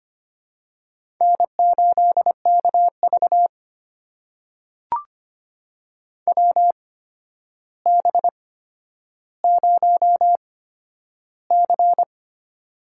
Morse code, heard as a keyed tone, about 25 wpm, spaced as for 11 wpm, 700 Hz. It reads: N8KV WB0C
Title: USA Call Signs - 4 Characters - 8x Character Spacing - 25wpm